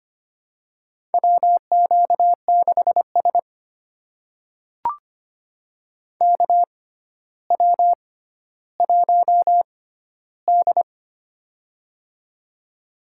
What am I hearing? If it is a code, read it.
Morse code, heard as a keyed tone, about 25 wpm, spaced as for 13 wpm, 700 Hz. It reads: WQ6S KW1D
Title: USA Call Signs - 4 Characters - 6x Character Spacing - 25wpm